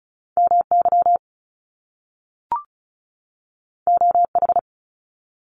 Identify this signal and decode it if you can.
Morse code, heard as a keyed tone, about 36 wpm, 700 Hz. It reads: MY OH
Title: Top Two Letter Combinations in Words 35wpm